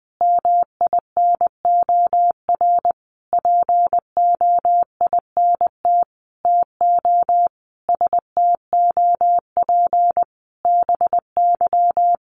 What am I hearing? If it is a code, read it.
Morse code, heard as a keyed tone, about 20 wpm, 700 Hz. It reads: MINOR POINT TO STOP BY
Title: Sentences from Top 1500 Words 20wpm